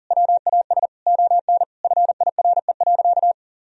Morse code, heard as a keyed tone, about 40 wpm, 700 Hz. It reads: WAS ON FIRE.
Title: Sentences from Top 1000 Words - Review Part 3 - 40wpm